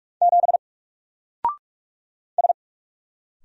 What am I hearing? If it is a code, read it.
Morse code, heard as a keyed tone, about 44 wpm, 700 Hz. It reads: 7 S
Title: Single Character (Letter:Number:Punctuation:Prosign) 45wpm